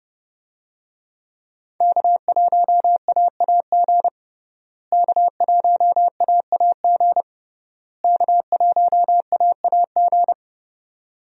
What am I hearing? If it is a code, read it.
Morse code, heard as a keyed tone, about 30 wpm, 700 Hz. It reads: K1AAG K1AAG K1AAG
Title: USA Call Signs - 5 Characters - Repeated 3x - 30wpm